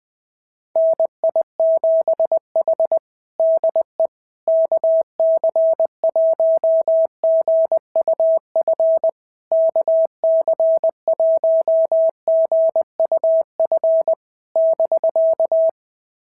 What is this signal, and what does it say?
Morse code, heard as a keyed tone, about 20 wpm, 650 Hz. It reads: NI7H DE KC1GUF KC1GUF <BK>
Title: SOTA QSO Practice 20wpm